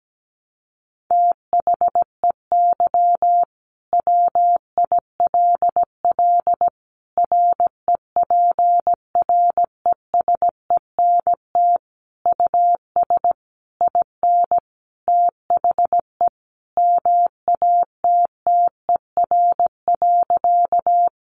Morse code, heard as a keyed tone, about 17 wpm, 700 Hz. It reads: THEY WILL REPRESENT US IN THE MATTER.
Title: Sentences from Top 1200 Words 17wpm